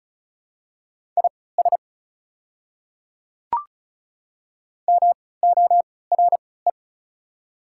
Morse code, heard as a keyed tone, about 36 wpm, spaced as for 20 wpm, 690 Hz. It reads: IS MORE
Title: 3x Character Spacing - Top 100 Words - 35wpm